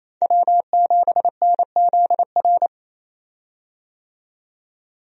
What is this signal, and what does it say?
Morse code, heard as a keyed tone, about 27 wpm, 700 Hz. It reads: W7NZR
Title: USA Call Signs - 5 Characters - 28wpm